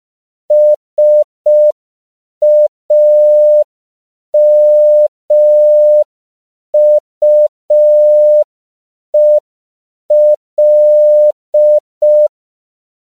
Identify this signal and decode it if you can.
Morse code, heard as a keyed tone, about 5 wpm, 600 Hz. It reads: SAMUEL